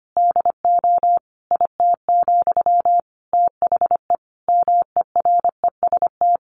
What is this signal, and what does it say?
Morse code, heard as a keyed tone, about 25 wpm, 700 Hz. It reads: DO IT, THE MEREST
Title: A Sound Of Thunder 25wpm